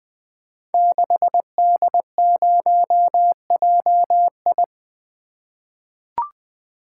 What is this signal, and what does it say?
Morse code, heard as a keyed tone, about 20 wpm, 700 Hz. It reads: 6D0JI
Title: Code Groups - 5 Letter:Numbers - 20wpm